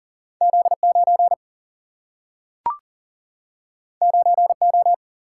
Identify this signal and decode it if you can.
Morse code, heard as a keyed tone, about 40 wpm, 700 Hz. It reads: Z 9 9 O
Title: Single-Letter Single-Number 40wpm